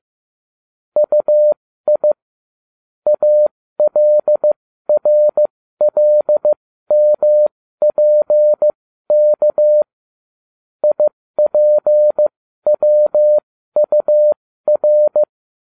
Morse code, heard as a keyed tone, about 15 wpm, spaced as for 13 wpm, 600 Hz. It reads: UI ALRLMPK IPWUR